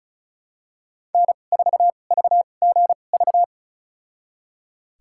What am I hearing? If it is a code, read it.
Morse code, heard as a keyed tone, about 34 wpm, spaced as for 25 wpm, 700 Hz. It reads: N4VGV
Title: USA Call Signs - 5 Characters - 2x Character Spacing - 35wpm